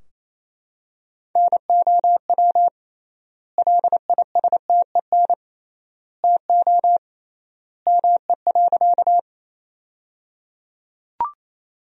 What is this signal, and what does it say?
Morse code, heard as a keyed tone, about 28 wpm, 700 Hz. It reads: NOW LISTEN TO ME.
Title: 3x Word-Spacing - Sentences from Top 400 Words - 28wpm